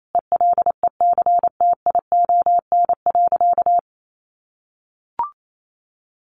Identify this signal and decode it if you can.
Morse code, heard as a keyed tone, about 28 wpm, 700 Hz. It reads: ELECTION.
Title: Sentences from Top 1500 Words 28wpm